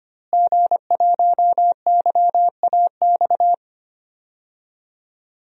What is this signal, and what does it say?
Morse code, heard as a keyed tone, about 25 wpm, 700 Hz. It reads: G1YAX
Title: DXCC-100 Call Signs - 5 Characters - Repeated 3x - 25wpm